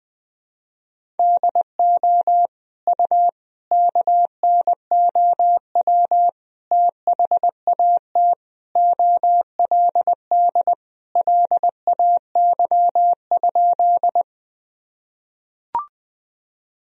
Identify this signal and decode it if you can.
Morse code, heard as a keyed tone, about 20 wpm, 700 Hz. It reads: DO U KNOW THAT OLD LAY?
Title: Sentences from Top 400 Words and Abbreviations 20wpm